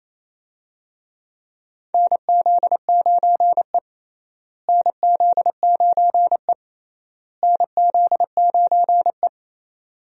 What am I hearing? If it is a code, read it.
Morse code, heard as a keyed tone, about 28 wpm, 700 Hz. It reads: NZ9E NZ9E NZ9E
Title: USA Call Signs - 4 Characters - Repeated 3x - 28wpm